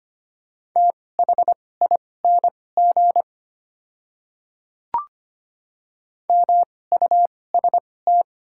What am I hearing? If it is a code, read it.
Morse code, heard as a keyed tone, about 25 wpm, spaced as for 18 wpm, 700 Hz. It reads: THING MUST
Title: 2x Character Spacing - Top 200 Words - 25wpm